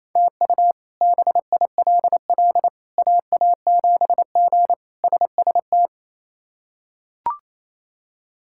Morse code, heard as a keyed tone, about 27 wpm, 700 Hz. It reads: TU BILL AA7G SST
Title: K1USN SST QSO Practice - 28wpm